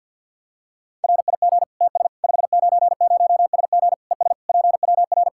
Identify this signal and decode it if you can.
Morse code, heard as a keyed tone, about 52 wpm, 700 Hz. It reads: RIG TS 590SG ES PWR
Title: Practice QSO - Exchange 2 - Info - 50wpm